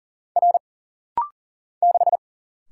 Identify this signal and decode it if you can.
Morse code, heard as a keyed tone, about 40 wpm, 700 Hz. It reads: R 6
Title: Single Letter-Number Alt - Rapid-Fire 40wpm